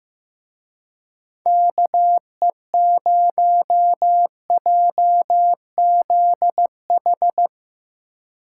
Morse code, heard as a keyed tone, about 17 wpm, 700 Hz. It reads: KE0JZH
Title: Speed Racing - USA 2x3 Call Signs 15wpm